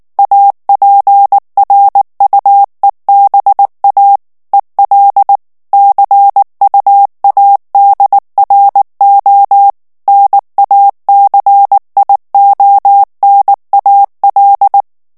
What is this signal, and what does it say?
Morse code, heard as a keyed tone, about 19 wpm, 800 Hz. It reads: APRUEBA EL CUADRO NACIONAL
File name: morse mp3 / ESPAGNOL TEXTE EN CLAIR 2 VITESSE 18 MOTS MINUTES